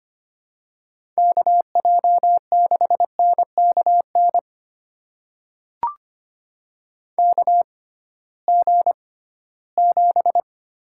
Morse code, heard as a keyed tone, about 25 wpm, spaced as for 16 wpm, 700 Hz. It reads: KJ6NKN KG7
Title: USA Call Signs - 6x Character Spacing - 25wpm